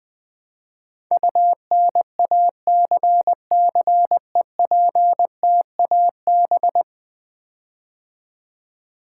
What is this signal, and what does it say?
Morse code, heard as a keyed tone, about 20 wpm, 700 Hz. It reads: UNACCEPTAB
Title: Build A Word - 5 Letters At A Time - 12-Letter Words - 20wpm